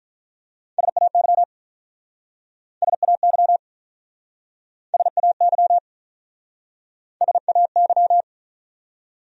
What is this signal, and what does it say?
Morse code, heard as a keyed tone, about 47 wpm, 700 Hz. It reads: SAY SAY SAY SAY